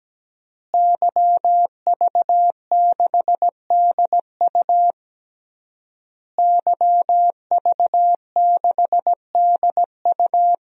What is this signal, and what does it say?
Morse code, heard as a keyed tone, about 17 wpm, 700 Hz. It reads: YV6DU YV6DU
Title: DXCC-100 Call Signs - 5 Characters - Repeated 3x - 17wpm